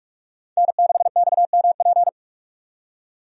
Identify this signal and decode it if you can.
Morse code, heard as a keyed tone, about 42 wpm, 700 Hz. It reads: N6XMP
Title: USA Call Signs - 5 Characters - 45wpm